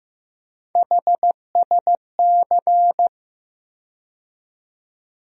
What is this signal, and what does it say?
Morse code, heard as a keyed tone, about 15 wpm, 700 Hz. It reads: HSC
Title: USA Call Sign Postfixes - 3 Letters 15wpm